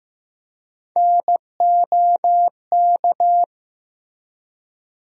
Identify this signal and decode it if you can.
Morse code, heard as a keyed tone, about 15 wpm, 700 Hz. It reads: NOK
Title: USA Call Sign Postfixes - 3 Letters 15wpm